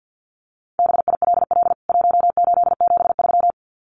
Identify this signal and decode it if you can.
Morse code, heard as a keyed tone, about 48 wpm, 700 Hz. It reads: BILL 1873
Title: CWT Exchange 50wpm